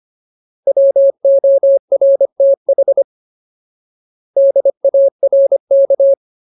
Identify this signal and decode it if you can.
Morse code, heard as a keyed tone, about 25 wpm, 550 Hz. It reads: WORTH DARK
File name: Ultimate Word List / 25WPM Ultimate